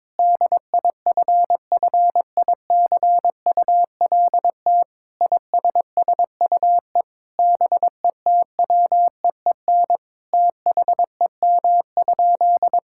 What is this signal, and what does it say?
Morse code, heard as a keyed tone, about 22 wpm, 700 Hz. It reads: DIFFICULT ISSUE BETWEEN THEM?
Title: Sentences from Top 1000 Words - Review Part 6 - 22wpm